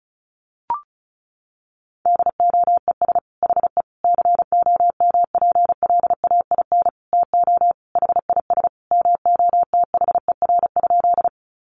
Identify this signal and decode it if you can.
Morse code, heard as a keyed tone, about 34 wpm, 700 Hz. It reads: DOES HE COMPLAIN TO HIS MOTHER?
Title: Sentences from Top 1300 Words 35wpm